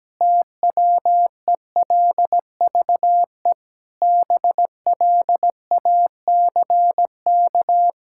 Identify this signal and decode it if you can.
Morse code, heard as a keyed tone, about 17 wpm, 700 Hz. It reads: TWELVE BLACK
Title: Pangram Challenge v2 - 17wpm